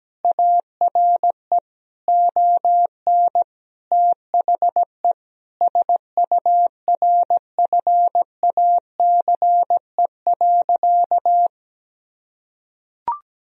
Morse code, heard as a keyed tone, about 17 wpm, 700 Hz. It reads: ARE ON THE SURFACE.
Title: Sentences from Top 1000 Words - Review Part 3 - 17wpm